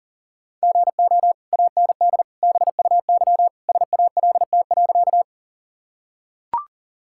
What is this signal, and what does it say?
Morse code, heard as a keyed tone, about 40 wpm, 700 Hz. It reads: GO AND BUY SALT.
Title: Sentences from Top 1000 Words - Review Part 5 - 40wpm